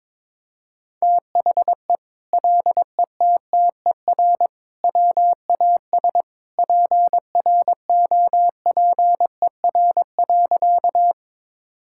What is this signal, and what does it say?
Morse code, heard as a keyed tone, about 22 wpm, 700 Hz. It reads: THE LETTER WAS PROPER.
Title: Sentences from Top 1000 Words - Review Part 5 - 22wpm